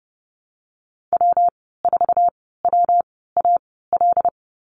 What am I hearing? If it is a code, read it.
Morse code, heard as a keyed tone, about 30 wpm, spaced as for 17 wpm, 700 Hz. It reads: W4WAL
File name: Lesson 19g - Letter w - Call Signs 30wpm